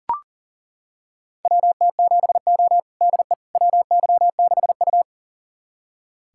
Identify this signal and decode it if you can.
Morse code, heard as a keyed tone, about 40 wpm, 700 Hz. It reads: WT7O DE WY6U
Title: USA Callsign DE Callsign - 4 Characters - 40wpm